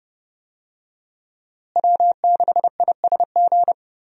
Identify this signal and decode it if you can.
Morse code, heard as a keyed tone, about 31 wpm, 700 Hz. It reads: W6ISG
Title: USA Call Signs - 5 Characters - Repeated 2x - 30wpm